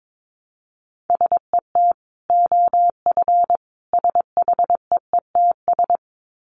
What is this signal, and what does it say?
Morse code, heard as a keyed tone, about 22 wpm, 700 Hz. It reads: SET OF SHEETS
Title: Dit Challenge 22wpm